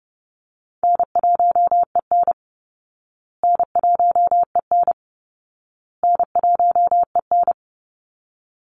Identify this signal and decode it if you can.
Morse code, heard as a keyed tone, about 30 wpm, 700 Hz. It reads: N1EN N1EN N1EN
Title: Easy Call Signs Repeated 3x - 30wpm